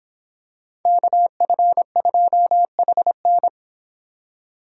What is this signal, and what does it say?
Morse code, heard as a keyed tone, about 26 wpm, 700 Hz. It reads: KF2HN